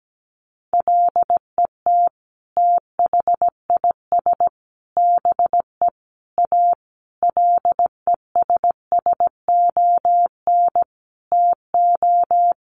Morse code, heard as a keyed tone, about 17 wpm, 700 Hz. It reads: LET THIS BE A LESSON TO
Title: Sentences from Top 1200 Words 17wpm